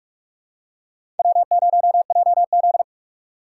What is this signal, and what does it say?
Morse code, heard as a keyed tone, about 46 wpm, 700 Hz. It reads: W0JZ
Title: USA Call Signs - 6x Character Spacing - 45wpm